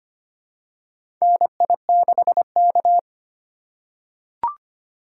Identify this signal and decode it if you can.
Morse code, heard as a keyed tone, about 25 wpm, 700 Hz. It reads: NI6K
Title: USA Call Signs - 4 Characters - 2x Character Spacing - 25wpm